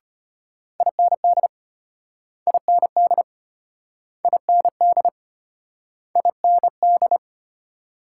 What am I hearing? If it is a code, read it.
Morse code, heard as a keyed tone, about 34 wpm, 700 Hz. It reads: IND IND IND IND